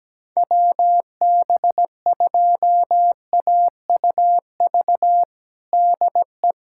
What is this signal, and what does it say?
Morse code, heard as a keyed tone, about 17 wpm, 700 Hz. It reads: WB2AUV DE
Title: USA Callsign DE Callsign - 6 Characters - 17wpm